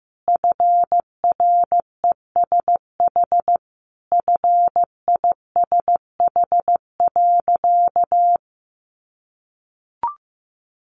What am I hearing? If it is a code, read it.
Morse code, heard as a keyed tone, about 14 wpm, 700 Hz. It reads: FRESH FISH.
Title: Sentences from Top 1500 Words 15wpm